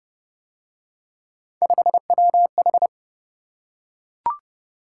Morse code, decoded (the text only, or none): 5WH